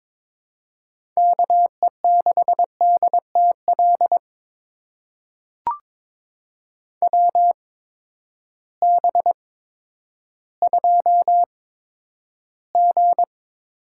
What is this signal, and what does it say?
Morse code, heard as a keyed tone, about 22 wpm, spaced as for 11 wpm, 700 Hz. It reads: KE6DTL WB2G
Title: USA Call Signs - 6 Characters - 8x Character Spacing - 22wpm